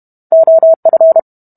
Morse code, heard as a keyed tone, about 32 wpm, 650 Hz. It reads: OF